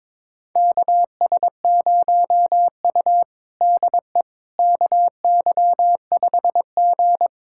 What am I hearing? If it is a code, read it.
Morse code, heard as a keyed tone, about 22 wpm, 700 Hz. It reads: KS0U DE KY5G
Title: USA Callsign DE Callsign - 4 Characters - 22wpm